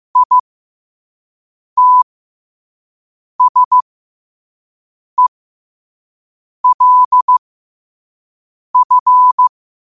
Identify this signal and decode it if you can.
Morse code, heard as a keyed tone, about 15 wpm, spaced as for 5 wpm, 1000 Hz. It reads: ITSELF
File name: qotdfiles / 5wpm